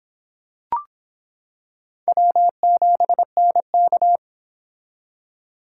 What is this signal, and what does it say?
Morse code, heard as a keyed tone, about 25 wpm, 700 Hz. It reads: W7NK